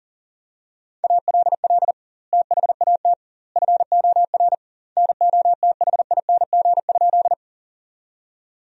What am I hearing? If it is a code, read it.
Morse code, heard as a keyed tone, about 37 wpm, 700 Hz. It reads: ALL THAT FOR NOTHING?
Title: 2x Word-Spacing - Sentences from Top 500 Words - 40wpm